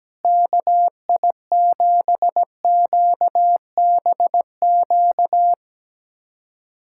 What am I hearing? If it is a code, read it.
Morse code, heard as a keyed tone, about 17 wpm, 700 Hz. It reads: KI7QBQ
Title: USA Callsign DE Callsign 17wpm